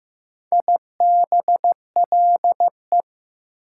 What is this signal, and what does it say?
Morse code, heard as a keyed tone, about 16 wpm, 700 Hz. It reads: IBLE
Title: Prefixes and Suffixes 15wpm